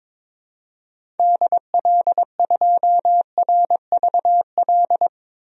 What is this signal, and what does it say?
Morse code, heard as a keyed tone, about 22 wpm, 700 Hz. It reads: DL2RVL
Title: DXCC-100 Call Signs - 6 Characters - 22wpm